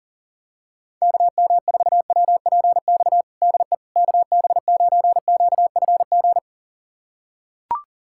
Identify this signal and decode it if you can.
Morse code, heard as a keyed tone, about 40 wpm, 700 Hz. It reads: KM4WPX DE KB9QFG
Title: USA Callsign DE Callsign - 6 Characters - 40wpm